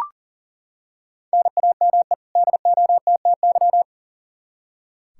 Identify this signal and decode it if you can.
Morse code, heard as a keyed tone, about 40 wpm, 700 Hz. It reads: NAME DOTTY
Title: K1USN SST QSO Practice 40wpm